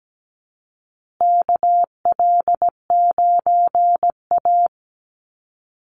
Morse code, heard as a keyed tone, about 17 wpm, 700 Hz. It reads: KL9A
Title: Call Signs Repeated 2x - 17wpm